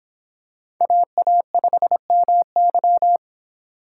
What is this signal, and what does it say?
Morse code, heard as a keyed tone, about 25 wpm, 700 Hz. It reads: AA5MY